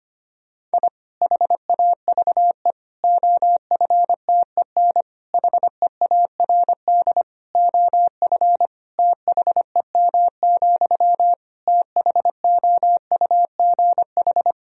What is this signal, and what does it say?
Morse code, heard as a keyed tone, about 25 wpm, 700 Hz. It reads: I HAVE OFTEN HEARD OF THEM, THOUGH
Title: Sentences from Top 1000 Words - Review Part 2 - 25wpm